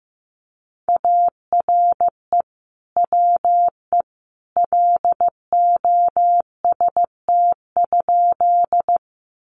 Morse code, heard as a keyed tone, about 15 wpm, 700 Hz. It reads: ARE WE LOST?
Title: Sentences from Top 1400 Words 15wpm